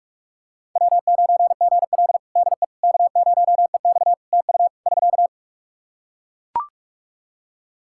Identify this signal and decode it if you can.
Morse code, heard as a keyed tone, about 42 wpm, 700 Hz. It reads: W9GL DE K0EX TU <SK>
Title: Practice QSO - Exchange 4 - Ending - 45wpm